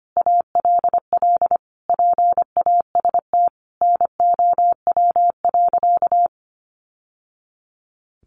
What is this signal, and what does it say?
Morse code, heard as a keyed tone, about 25 wpm, 700 Hz. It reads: ALL PAST NOW.
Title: Sentences from Top 600 Words 25wpm